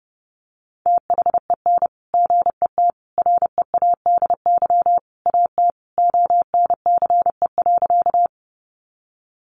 Morse code, heard as a keyed tone, about 30 wpm, 700 Hz. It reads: THEN GET READY AT ONCE.
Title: Sentences from Top 400 Words 30wpm